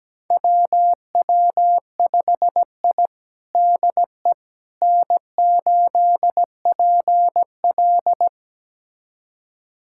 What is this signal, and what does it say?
Morse code, heard as a keyed tone, about 17 wpm, 700 Hz. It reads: WW5I DE N8PL
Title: USA Callsign DE Callsign - 4 Characters - 17wpm